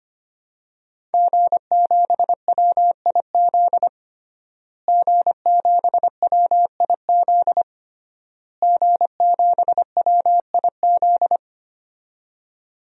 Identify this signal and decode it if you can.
Morse code, heard as a keyed tone, about 25 wpm, 700 Hz. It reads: G7WIZ G7WIZ G7WIZ
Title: DXCC-100 Call Signs - 5 Characters - Repeated 3x - 25wpm